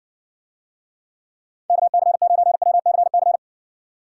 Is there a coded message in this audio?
Yes: Morse code from 1.7 to 3.4 s, heard as a keyed tone, about 56 wpm, 700 Hz. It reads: KC9WCC